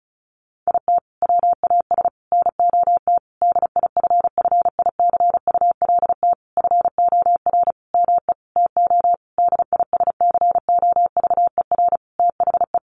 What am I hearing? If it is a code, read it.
Morse code, heard as a keyed tone, about 34 wpm, 700 Hz. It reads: IT WAS NOT DIFFICULT FOR ME TO DISCOVER THE
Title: Sentences from Top 1200 Words 35wpm